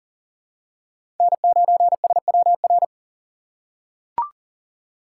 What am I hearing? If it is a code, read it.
Morse code, heard as a keyed tone, about 40 wpm, 700 Hz. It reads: N9SWR
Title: USA Call Signs - 5 Characters - 2x Character Spacing - 40wpm